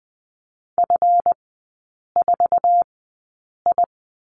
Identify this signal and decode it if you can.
Morse code, heard as a keyed tone, about 20 wpm, spaced as for 8 wpm, 700 Hz. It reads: F4I
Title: 138 Lesson 23e - Letter f - Three Characters 20wpm